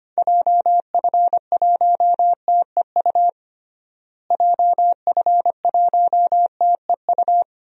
Morse code, heard as a keyed tone, about 25 wpm, 700 Hz. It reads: JF1TEU JF1TEU
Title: DXCC-100 Call Signs - 6 Characters - Repeated 2x - 25wpm